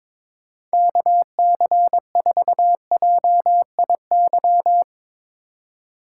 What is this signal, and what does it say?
Morse code, heard as a keyed tone, about 22 wpm, 700 Hz. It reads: KC4JIY
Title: USA Call Signs - 6 Characters - 4x Character Spacing - 22wpm